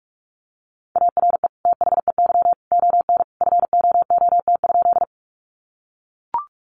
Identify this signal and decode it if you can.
Morse code, heard as a keyed tone, about 44 wpm, 700 Hz. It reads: ARE THEY ON FOOT?
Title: Sentences from Top 500 Words 45wpm